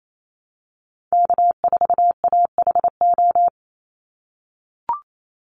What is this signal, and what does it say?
Morse code, heard as a keyed tone, about 28 wpm, 700 Hz. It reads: K4AHO
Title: Call Signs - 28wpm